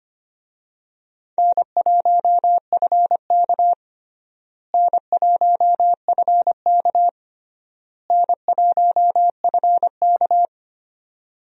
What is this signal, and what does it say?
Morse code, heard as a keyed tone, about 25 wpm, 700 Hz. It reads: N1FK N1FK N1FK
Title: USA Call Signs - 4 Characters - Repeated 3x - 25wpm